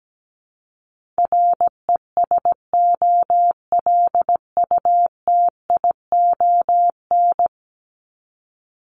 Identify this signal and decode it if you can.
Morse code, heard as a keyed tone, about 17 wpm, 700 Hz. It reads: RESOLUTION
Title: Top 1501-1750 Words 17wpm